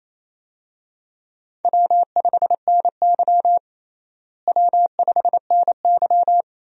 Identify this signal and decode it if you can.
Morse code, heard as a keyed tone, about 28 wpm, 700 Hz. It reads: W5NY W5NY
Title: USA Call Signs - 4 Characters - Repeated 3x - 28wpm